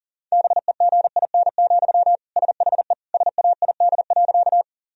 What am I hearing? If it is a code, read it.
Morse code, heard as a keyed tone, about 37 wpm, 700 Hz. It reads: BEGIN, SHE SAID.
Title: Sentences from Top 1000 Words - Review Part 1 - 40wpm